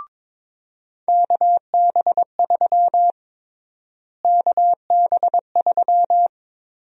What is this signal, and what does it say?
Morse code, heard as a keyed tone, about 22 wpm, 700 Hz. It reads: KB3 KB3
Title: USA Call Sign Prefixes - Repeated 2x - 22wpm